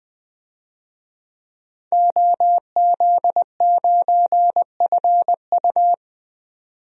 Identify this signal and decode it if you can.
Morse code, heard as a keyed tone, about 20 wpm, 700 Hz. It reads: OZ9FU
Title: DXCC-100 Call Signs - 5 Characters - 20wpm